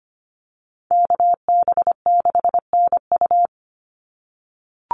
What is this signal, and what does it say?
Morse code, heard as a keyed tone, about 25 wpm, 700 Hz. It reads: KB6NU
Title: Call Signs - 25wpm